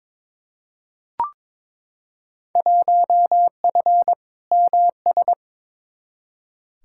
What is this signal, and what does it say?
Morse code, heard as a keyed tone, about 21 wpm, 700 Hz. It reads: 1F MS
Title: Field Day Exchanges Encore - 22wpm